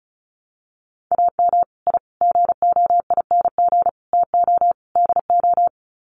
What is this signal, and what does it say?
Morse code, heard as a keyed tone, about 34 wpm, 700 Hz. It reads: AM I GOING TO DO
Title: Sets of 5 Words Encore 35wpm